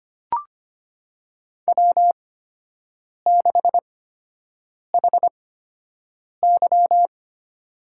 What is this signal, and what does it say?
Morse code, heard as a keyed tone, about 25 wpm, spaced as for 7 wpm, 700 Hz. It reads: W6HY
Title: USA Call Signs - 4 Characters - 8x Character Spacing - 25wpm